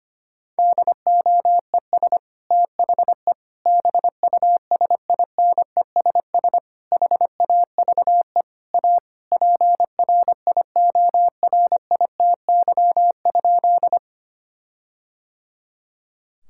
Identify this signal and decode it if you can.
Morse code, heard as a keyed tone, about 25 wpm, 700 Hz. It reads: DOES THE BUSINESS HAVE A PRIORITY?
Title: Sentences from Top 1600 Words 25wpm